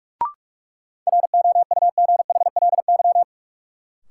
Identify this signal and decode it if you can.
Morse code, heard as a keyed tone, about 46 wpm, 700 Hz. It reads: ROUGHLY
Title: Sentences from Top 1500 Words - New Word Drill Alt - 45wpm